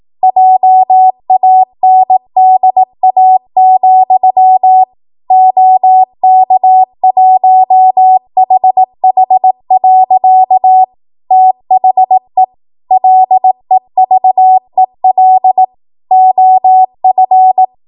Morse code, heard as a keyed tone, about 18 wpm, 750 Hz. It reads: JANDA, OK1HH. THE LEVEL OF
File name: Morse / 18WPM